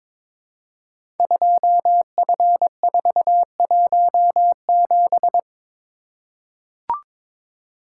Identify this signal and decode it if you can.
Morse code, heard as a keyed tone, about 22 wpm, 700 Hz. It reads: 2F417
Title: Code Groups - 5 Letter:Numbers - 22wpm